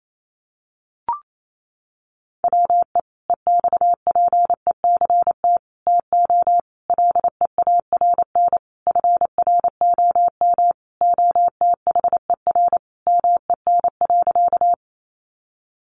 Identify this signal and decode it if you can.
Morse code, heard as a keyed tone, about 28 wpm, 700 Hz. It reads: WE EXPECT TO LEARN FROM OTHER MEN.
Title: Sentences from Top 1100 Words 28wpm